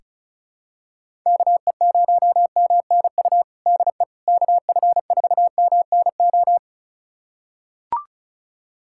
Morse code, heard as a keyed tone, about 34 wpm, 700 Hz. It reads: KE0MNU DE KF4MNO
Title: USA Callsign DE Callsign - 6 Characters - 35wpm